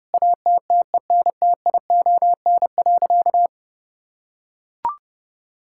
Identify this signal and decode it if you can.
Morse code, heard as a keyed tone, about 30 wpm, 700 Hz. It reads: ATTENTION.
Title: Sentences from Top 1000 Words - Review Part 6 - 30wpm